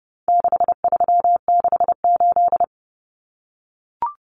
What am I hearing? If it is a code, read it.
Morse code, heard as a keyed tone, about 30 wpm, 700 Hz. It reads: 6368
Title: Common Cut Numbers - 4-Digit 30wpm